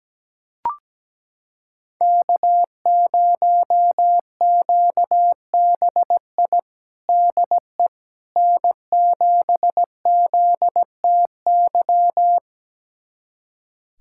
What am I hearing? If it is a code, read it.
Morse code, heard as a keyed tone, about 17 wpm, 700 Hz. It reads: K0QBI DE N7ZTY
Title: USA Callsign DE Callsign - 5 Characters - 17wpm